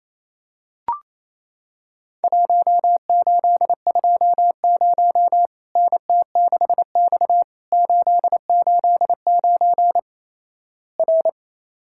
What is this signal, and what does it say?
Morse code, heard as a keyed tone, about 27 wpm, 700 Hz. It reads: 1820 NT6X 889 R
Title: Worked All Europe DX Contest - QTC Exchange - 28wpm